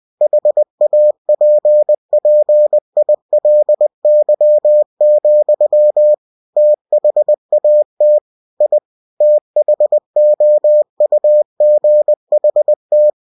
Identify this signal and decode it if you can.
Morse code, heard as a keyed tone, about 20 wpm, 600 Hz. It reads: HAPPILY, THAT I THOUGHT
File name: cw-books / winnie-the-pooh / winnie-20wpm-page-0058